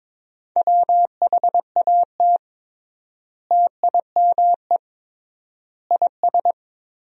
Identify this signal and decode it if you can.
Morse code, heard as a keyed tone, about 22 wpm, 700 Hz. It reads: WHAT TIME IS
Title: 3x Word-Spacing - Sentences from Top 100 Words - 22wpm